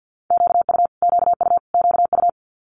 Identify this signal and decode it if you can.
Morse code, heard as a keyed tone, about 50 wpm, 700 Hz. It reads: QU QU QU
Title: Multi-letter Phonemes Repeated 3x 50wpm